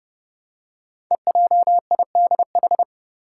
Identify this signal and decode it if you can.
Morse code, heard as a keyed tone, about 30 wpm, 690 Hz. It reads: EJIDH
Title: Code Groups - 5 Letters - 30wpm